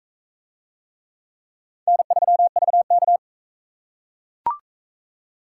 Morse code, heard as a keyed tone, about 43 wpm, 700 Hz. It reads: N3VK